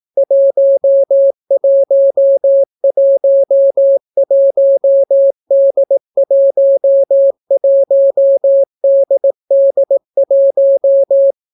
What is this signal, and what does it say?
Morse code, heard as a keyed tone, about 18 wpm, 550 Hz. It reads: 1111D11DD1